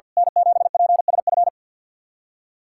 Morse code, heard as a keyed tone, about 52 wpm, 700 Hz. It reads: N7PSF
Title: USA Callsign DE Callsign - 5 Characters - 50wpm